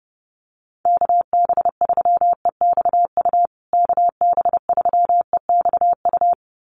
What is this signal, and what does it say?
Morse code, heard as a keyed tone, about 30 wpm, 700 Hz. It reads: KB3EXU KB3EXU
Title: USA Call Signs Repeated 2x 30wpm